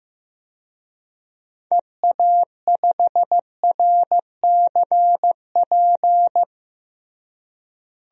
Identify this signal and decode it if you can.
Morse code, heard as a keyed tone, about 15 wpm, 700 Hz. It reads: EA5RCP
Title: DXCC-100 Call Signs - 6 Characters - Repeated 3x - 15wpm